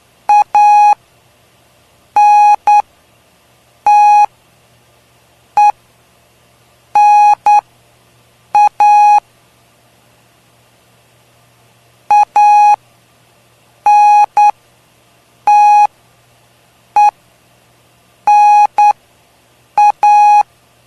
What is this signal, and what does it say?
Morse code, heard as a keyed tone, about 9 wpm, spaced as for 6 wpm, 830 Hz. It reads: ANTENA ANTENA
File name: cw / py2lcd / 03-PY2LCD-Li��o 3